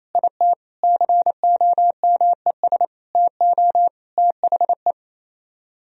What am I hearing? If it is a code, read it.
Morse code, heard as a keyed tone, about 34 wpm, 700 Hz. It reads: IT COMES TO THE
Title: Speed-Racing - Sets of 5 Words 28wpm